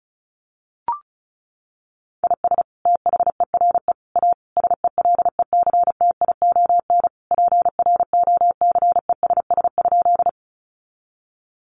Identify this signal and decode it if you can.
Morse code, heard as a keyed tone, about 34 wpm, 700 Hz. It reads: IS THERE A SELECTION PROCESS?
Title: Sentences from Top 1400 Words 35wpm